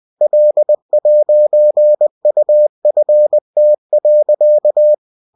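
Morse code, heard as a keyed tone, about 20 wpm, 600 Hz. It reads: L'UFT.
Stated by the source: Bulletin sent in CW.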